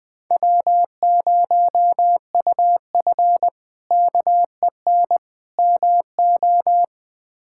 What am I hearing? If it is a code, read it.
Morse code, heard as a keyed tone, about 20 wpm, 700 Hz. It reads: W0UF KEN MO
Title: K1USN SST QSO Practice - 20wpm